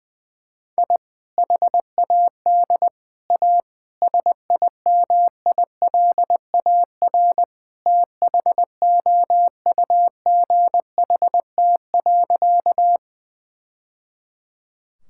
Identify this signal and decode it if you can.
Morse code, heard as a keyed tone, about 21 wpm, 700 Hz. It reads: I HAD A SIMILAR THOUGHT.
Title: Sentences from Top 1000 Words - Review Part 5 - 20wpm